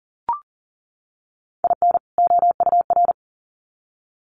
Morse code, heard as a keyed tone, about 40 wpm, 700 Hz. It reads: IN OUR
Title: Sets of 2 Words 40wpm